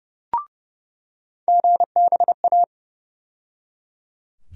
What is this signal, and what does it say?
Morse code, heard as a keyed tone, about 30 wpm, 700 Hz. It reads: GBA
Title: USA Call Sign Postfixes - 3 Letters 30wpm